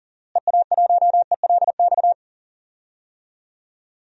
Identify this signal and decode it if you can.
Morse code, heard as a keyed tone, about 40 wpm, 700 Hz. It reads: EA1ELX
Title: DXCC-100 Call Signs - 6 Characters - Repeated 2x - 40wpm